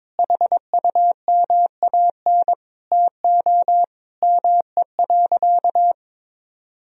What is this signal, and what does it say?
Morse code, heard as a keyed tone, about 22 wpm, 700 Hz. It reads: HUMAN TO ME.
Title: Sentences from Top 1000 Words - Review Part 3 - 22wpm